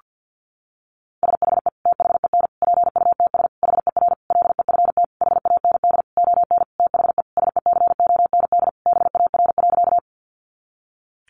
Morse code, heard as a keyed tone, about 50 wpm, 700 Hz. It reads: SHE THEN PUTS HER LEFT HAND ON THE SECOND BAR.
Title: Easy Phrases 50wpm